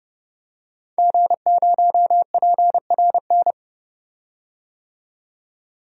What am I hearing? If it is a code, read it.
Morse code, heard as a keyed tone, about 30 wpm, 700 Hz. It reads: G0PRN
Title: DXCC-100 Call Signs - 5 Characters - 30wpm